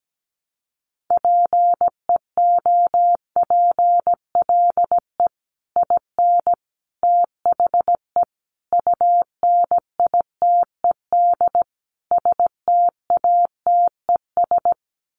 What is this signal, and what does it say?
Morse code, heard as a keyed tone, about 17 wpm, 700 Hz. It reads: PEOPLE IN THE UNITED STATES
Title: Sets of 5 Words Encore 17wpm